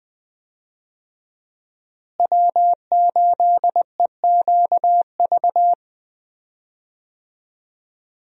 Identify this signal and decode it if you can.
Morse code, heard as a keyed tone, about 20 wpm, 700 Hz. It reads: W8EQV
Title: USA Call Signs - 5 Characters - Repeated 3x - 20wpm